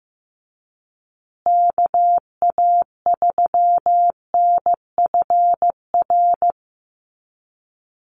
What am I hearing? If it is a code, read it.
Morse code, heard as a keyed tone, about 15 wpm, 700 Hz. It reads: KA3NFR
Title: USA Call Signs Repeated 3x 15wpm